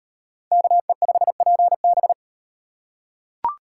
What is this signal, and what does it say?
Morse code, heard as a keyed tone, about 36 wpm, 700 Hz. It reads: KE5PB